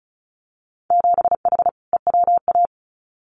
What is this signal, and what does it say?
Morse code, heard as a keyed tone, about 36 wpm, 700 Hz. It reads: 7H EWA
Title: Winter Field Day Exchanges Encore 35wpm